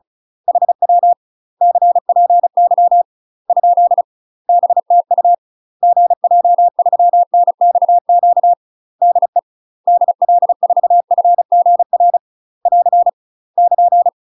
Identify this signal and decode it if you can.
Morse code, heard as a keyed tone, about 34 wpm, 700 Hz. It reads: HW CPY ? BTU GJ3NXQ DE DL4FGR <AR> <KN>
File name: UK MASTER QSOs TRAINING / 35WPM / UK MASTER QSOs 0001